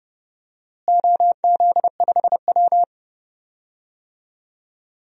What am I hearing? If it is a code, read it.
Morse code, heard as a keyed tone, about 30 wpm, 700 Hz. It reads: OZ5W
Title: DXCC-100 Call Signs - 4 Characters - Repeated 2x - 30wpm